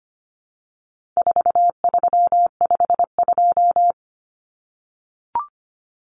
Morse code, heard as a keyed tone, about 25 wpm, 700 Hz. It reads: 4352
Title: Common Cut Numbers - 4-Digit 25wpm